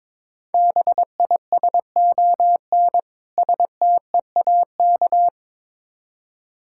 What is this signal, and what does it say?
Morse code, heard as a keyed tone, about 22 wpm, 690 Hz. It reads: BISON STEAK
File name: Difficult Words with Hints - 22wpm